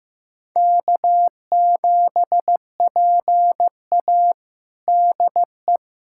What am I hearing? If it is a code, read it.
Morse code, heard as a keyed tone, about 15 wpm, 700 Hz. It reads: K7PA DE
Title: USA Callsign DE Callsign - 4 Characters - 15wpm